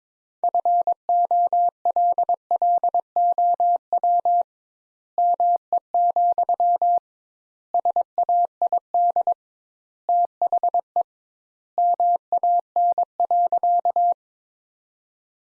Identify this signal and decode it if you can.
Morse code, heard as a keyed tone, about 22 wpm, 700 Hz. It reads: FOLLOW ME, SAID THE MAN.
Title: Sentences Top 200 Words - 22wpm